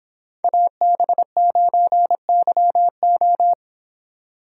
Morse code, heard as a keyed tone, about 26 wpm, 700 Hz. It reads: AB9YO